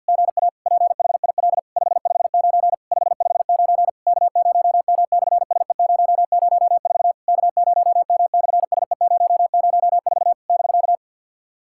Morse code, heard as a keyed tone, about 50 wpm, 690 Hz. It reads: GA PHIL 559 559 K0M/SE004 K0M/SE004 <BK>
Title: SOTA QSO Practice 50wpm